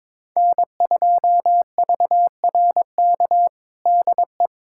Morse code, heard as a keyed tone, about 22 wpm, 700 Hz. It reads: N2VRK DE
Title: USA Callsign DE Callsign - 5 Characters - 22wpm